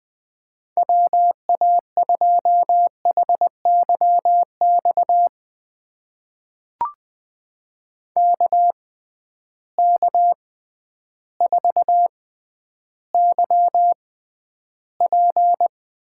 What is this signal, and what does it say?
Morse code, heard as a keyed tone, about 20 wpm, spaced as for 11 wpm, 700 Hz. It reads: WA2HYX KK4YP
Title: USA Call Signs - 6 Characters - 6x Character Spacing - 20wpm